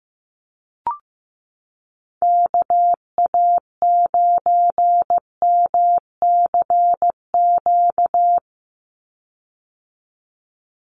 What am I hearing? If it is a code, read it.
Morse code, heard as a keyed tone, about 15 wpm, 700 Hz. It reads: KA9MCQ
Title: USA Call Signs 15wpm